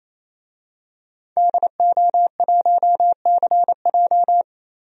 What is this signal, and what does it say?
Morse code, heard as a keyed tone, about 28 wpm, 700 Hz. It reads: DO1CJ
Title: DXCC-100 Call Signs - 5 Characters - 28wpm